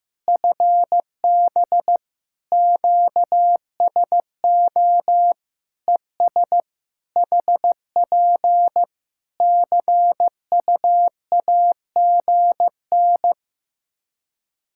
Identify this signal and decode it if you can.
Morse code, heard as a keyed tone, about 15 wpm, 700 Hz. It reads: FB QSO ES HP CUAGN
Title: Practice QSO - Exchange 4 - Ending - 15wpm